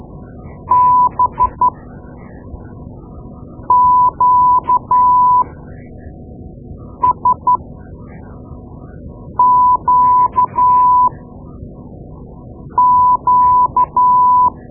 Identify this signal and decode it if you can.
Morse code, heard as a keyed tone, about 12 wpm, spaced as for 4 wpm, 1000 Hz. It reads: BQSQQ
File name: Morse Lessons / Morse Lessons / Tape-2A